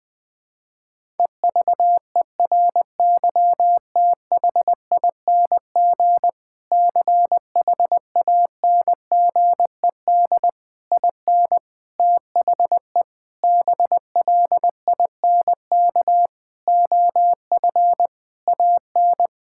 Morse code, heard as a keyed tone, about 19 wpm, 700 Hz. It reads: EVERYTHING CHANGED IN THE BLINK OF AN
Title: Sentences with 6-Word Phrases - Spoken Beginning - 20wpm